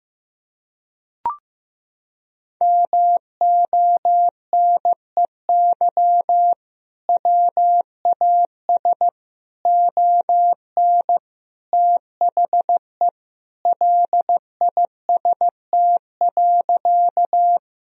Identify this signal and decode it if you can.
Morse code, heard as a keyed tone, about 16 wpm, 700 Hz. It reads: MONEY WAS ON THE LIST.
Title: Sentences from Top 1000 Words - Review Part 2 - 15wpm